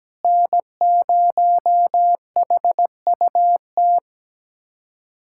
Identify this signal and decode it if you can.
Morse code, heard as a keyed tone, about 17 wpm, 700 Hz. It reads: N0HUT
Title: USA Callsign DE Callsign - 5 Characters - 17wpm